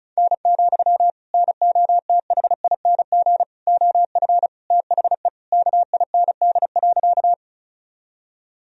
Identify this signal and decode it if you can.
Morse code, heard as a keyed tone, about 36 wpm, 700 Hz. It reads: N, NOTHING OF THE KIND.
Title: Sentences from Top 500 Words and Abbreviations 35wpm